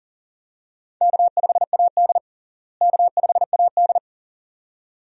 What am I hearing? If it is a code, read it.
Morse code, heard as a keyed tone, about 40 wpm, 700 Hz. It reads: K5AD K5AD
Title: USA Call Signs - 4 Characters - Repeated 2x - 40wpm